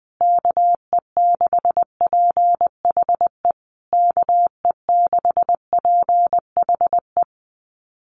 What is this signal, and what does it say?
Morse code, heard as a keyed tone, about 20 wpm, 700 Hz. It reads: KE6PHE KE6PHE
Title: USA Call Signs Repeated 3x 20wpm